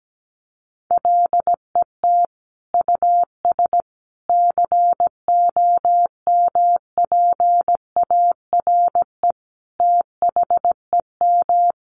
Morse code, heard as a keyed tone, about 17 wpm, 700 Hz. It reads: LET US COMPARE THEM
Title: Sentences from Top 1300 Words 17wpm